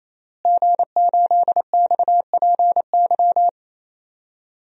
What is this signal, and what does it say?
Morse code, heard as a keyed tone, about 28 wpm, 700 Hz. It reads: G8XPY
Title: DXCC-100 Call Signs - 5 Characters - Repeated 3x - 28wpm